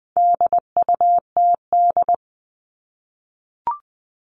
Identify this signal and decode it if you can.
Morse code, heard as a keyed tone, about 20 wpm, 700 Hz. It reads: DUTD
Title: All Cut Numbers - 4-Digit 20wpm